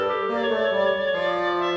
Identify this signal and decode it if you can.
There is no code.